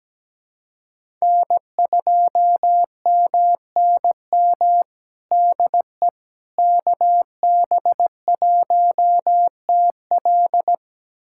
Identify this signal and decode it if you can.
Morse code, heard as a keyed tone, about 17 wpm, 700 Hz. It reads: N2MNM DE KB1TL
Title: USA Callsign DE Callsign - 5 Characters - 17wpm